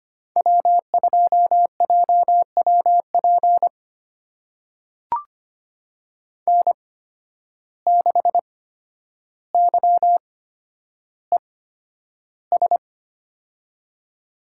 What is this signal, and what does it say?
Morse code, heard as a keyed tone, about 25 wpm, spaced as for 11 wpm, 700 Hz. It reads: W2JWP N6YES
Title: USA Call Signs - 5 Characters - 8x Character Spacing - 25wpm